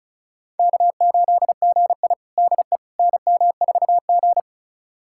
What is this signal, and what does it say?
Morse code, heard as a keyed tone, about 36 wpm, 700 Hz. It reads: K8GI DE NM4G
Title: USA Callsign DE Callsign - 4 Characters - 35wpm